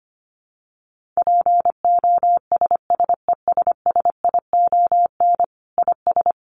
Morse code, heard as a keyed tone, about 25 wpm, 700 Hz. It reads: POSSESSION IS
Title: Sentences from Top 1400 Words 25wpm